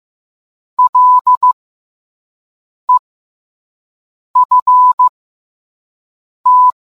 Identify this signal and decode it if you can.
Morse code, heard as a keyed tone, about 15 wpm, spaced as for 5 wpm, 1000 Hz. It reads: LEFT